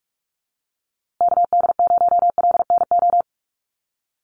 Morse code, heard as a keyed tone, about 44 wpm, 700 Hz. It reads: KD0LNO
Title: USA Call Signs Repeated 3x 45wpm